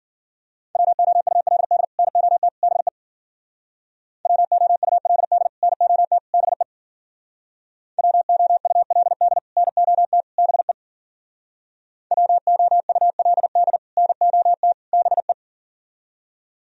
Morse code, heard as a keyed tone, about 51 wpm, 700 Hz. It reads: WOULD NOT BE WOULD NOT BE WOULD NOT BE WOULD NOT BE